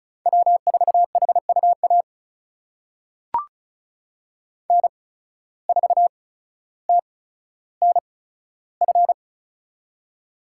Morse code, heard as a keyed tone, about 34 wpm, spaced as for 15 wpm, 700 Hz. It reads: W4HUA N4TNF
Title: USA Call Signs - 5 Characters - 8x Character Spacing - 35wpm